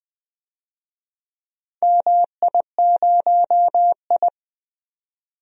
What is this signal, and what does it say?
Morse code, heard as a keyed tone, about 20 wpm, 700 Hz. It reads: MI0I
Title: DXCC-100 Call Signs - 4 Characters - 20wpm